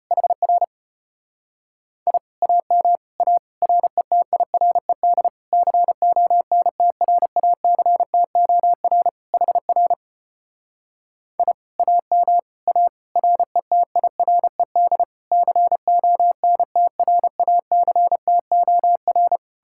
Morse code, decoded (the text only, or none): HR I AM A RETIRED CONTRACTOR HR I AM A RETIRED CONTRACTOR